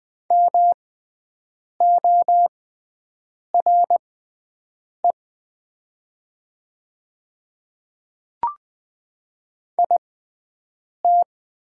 Morse code, heard as a keyed tone, about 20 wpm, spaced as for 7 wpm, 690 Hz. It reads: MORE IT
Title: 6x Character Spacing - Top 100 Words - 20wpm